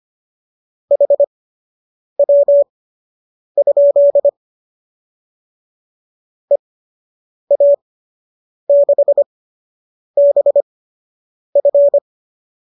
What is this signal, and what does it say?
Morse code, heard as a keyed tone, about 25 wpm, spaced as for 8 wpm, 570 Hz. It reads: HW? EA6BF